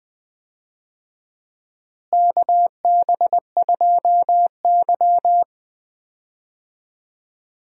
Morse code, heard as a keyed tone, about 20 wpm, 700 Hz. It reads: KB2Y
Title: USA Call Signs - 4 Characters - Repeated 3x - 20wpm